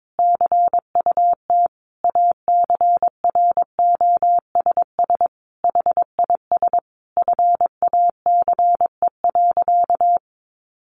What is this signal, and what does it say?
Morse code, heard as a keyed tone, about 22 wpm, 700 Hz. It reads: CUT ACROSS HIS FACE.
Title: Sentences from Top 700 Words 22wpm